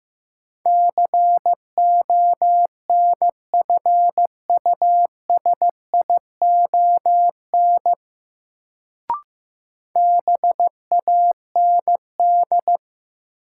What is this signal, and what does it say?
Morse code, heard as a keyed tone, about 15 wpm, 700 Hz. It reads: CONFUSION BAND
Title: Sentences from Top 1600 Words - New Word Drill - 15wpm